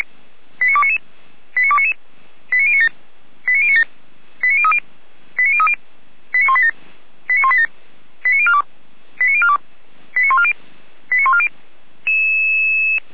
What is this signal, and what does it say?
Five-tone selective call (ZVEI), heard as quick runs of five tones, tones of about 70 ms. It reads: 89299 89299 89097 89097 89030 89030 89187 89187 89042 89042 89140 89140